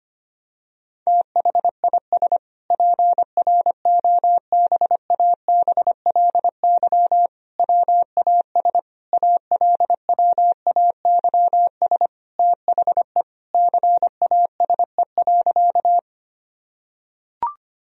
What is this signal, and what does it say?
Morse code, heard as a keyed tone, about 25 wpm, 700 Hz. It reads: THIS PROBABLY WAS ALWAYS THE CASE.
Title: Sentences from Top 1000 Words - Review Part 4 - 25wpm